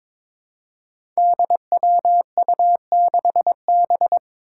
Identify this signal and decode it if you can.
Morse code, heard as a keyed tone, about 22 wpm, 700 Hz. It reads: DWU6B
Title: Five Challenging Letter-Number Combinations 22wpm